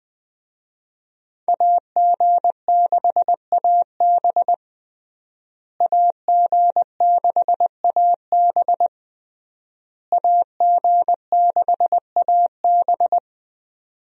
Morse code, heard as a keyed tone, about 21 wpm, 700 Hz. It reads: AG6AB AG6AB AG6AB
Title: USA Call Signs - 5 Characters - Repeated 3x - 20wpm